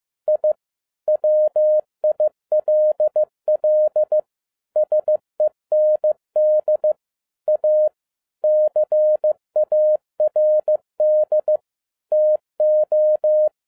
Morse code, heard as a keyed tone, about 15 wpm, 610 Hz. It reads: I WILL SEND A CARD TO
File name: 2011-10-mp3 / n07-A DOT 15wpm.mp3